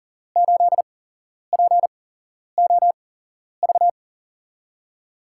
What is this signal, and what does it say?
Morse code, heard as a keyed tone, about 40 wpm, spaced as for 11 wpm, 700 Hz. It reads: 8POV
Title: USA Call Signs - 8x Character Spacing - 40wpm